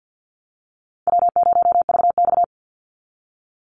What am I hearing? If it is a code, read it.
Morse code, heard as a keyed tone, about 50 wpm, 700 Hz. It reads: W0VX
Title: Call Signs Repeated 3x - 50wpm